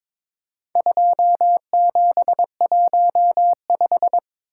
Morse code, heard as a keyed tone, about 22 wpm, 700 Hz. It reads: 2715
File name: practice / 22wpm / Four Letter-Number Combinations - 22wpm